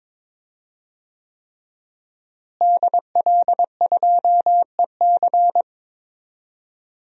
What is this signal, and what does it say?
Morse code, heard as a keyed tone, about 22 wpm, 700 Hz. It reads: DL2EC
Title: DXCC-100 Call Signs - 5 Characters - Repeated 3x - 22wpm